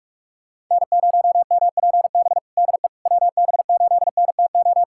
Morse code, heard as a keyed tone, about 46 wpm, 700 Hz. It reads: N0MPB DE WB8NTO
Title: USA Callsign DE Callsign 45wpm